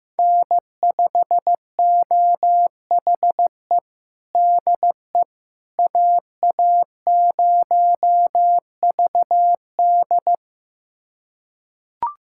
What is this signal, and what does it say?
Morse code, heard as a keyed tone, about 16 wpm, 700 Hz. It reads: N5OHE DE AA0VD
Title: USA Callsign DE Callsign - 5 Characters - 15wpm